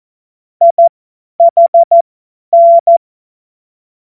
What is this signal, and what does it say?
Morse code, heard as a keyed tone, about 14 wpm, spaced as for 10 wpm, 680 Hz. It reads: IHN
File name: Morsekurs / Witze / Witz 13 Februar 2024 50 F70